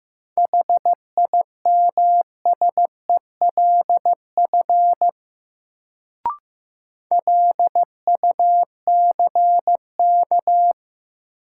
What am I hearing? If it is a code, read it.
Morse code, heard as a keyed tone, about 15 wpm, 700 Hz. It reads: HIMSELF LUCK
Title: Sentences from Top 1500 Words - New Word Drill Alt - 15wpm